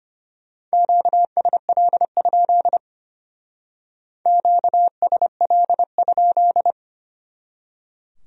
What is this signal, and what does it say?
Morse code, heard as a keyed tone, about 31 wpm, 700 Hz. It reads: QSL? QSL?
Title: Speed-Racing Q-Codes - 25wpm